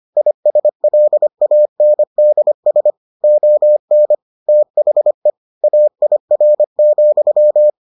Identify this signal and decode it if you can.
Morse code, heard as a keyed tone, about 25 wpm, 600 Hz. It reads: ISLANDS ON THE AIR,